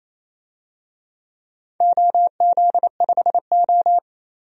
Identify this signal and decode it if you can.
Morse code, heard as a keyed tone, about 28 wpm, 700 Hz. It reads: OZ5O
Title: DXCC-100 Call Signs - 4 Characters - 28wpm